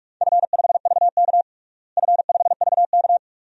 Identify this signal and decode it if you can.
Morse code, heard as a keyed tone, about 46 wpm, 700 Hz. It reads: F5VK F5VK
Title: DXCC-100 Call Signs - 4 Characters - Repeated 2x - 45wpm